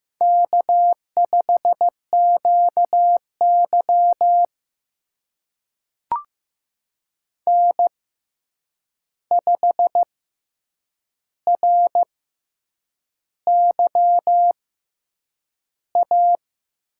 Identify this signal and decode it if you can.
Morse code, heard as a keyed tone, about 15 wpm, spaced as for 7 wpm, 700 Hz. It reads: K5QY N5RYA
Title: USA Call Signs - 6x Character Spacing - 15wpm